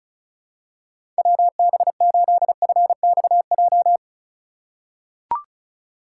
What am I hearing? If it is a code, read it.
Morse code, heard as a keyed tone, about 36 wpm, 700 Hz. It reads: WB8FXJ
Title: K1USN SST QSO Practice - 35wpm